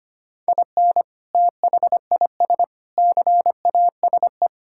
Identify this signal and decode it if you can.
Morse code, heard as a keyed tone, about 25 wpm, 700 Hz. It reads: IN THIS CASE
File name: practice / 25wpm / Sentences with 3-Word Phrases - 25wpm